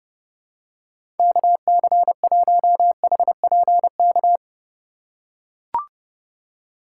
Morse code, heard as a keyed tone, about 30 wpm, 700 Hz. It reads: KC1HPK
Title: USA Call Signs - 6 Characters - 3x Character Spacing - 30wpm